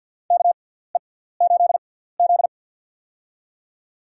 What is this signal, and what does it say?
Morse code, heard as a keyed tone, about 49 wpm, spaced as for 17 wpm, 700 Hz. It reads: KE8Z
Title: USA Call Signs - 4 Characters - 6x Character Spacing - 50wpm